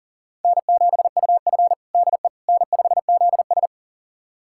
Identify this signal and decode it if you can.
Morse code, heard as a keyed tone, about 40 wpm, 700 Hz. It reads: N7UF DE N5ZS
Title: USA Callsign DE Callsign - 4 Characters - 40wpm